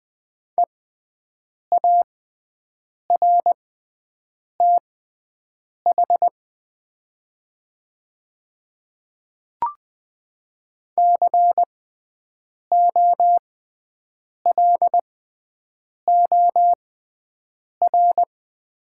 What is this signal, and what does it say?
Morse code, heard as a keyed tone, about 20 wpm, spaced as for 7 wpm, 700 Hz. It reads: EARTH COLOR
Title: 6x Character Spacing - Top 300 Words - 20wpm